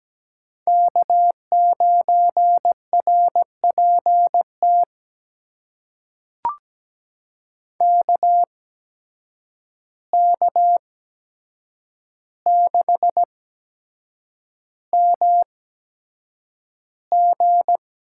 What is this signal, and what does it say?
Morse code, heard as a keyed tone, about 17 wpm, spaced as for 7 wpm, 700 Hz. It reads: K9RPT KK6MG
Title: USA Call Signs - 5 Characters - 8x Character Spacing - 17wpm